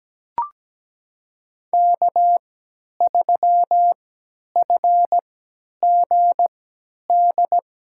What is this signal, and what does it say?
Morse code, heard as a keyed tone, about 17 wpm, spaced as for 10 wpm, 700 Hz. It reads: K3FGD
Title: USA Call Signs - 5 Characters - 3x Character Spacing - 17wpm